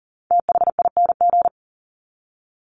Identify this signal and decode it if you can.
Morse code, heard as a keyed tone, about 40 wpm, 700 Hz. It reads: THING
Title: Top 300 Words 40wpm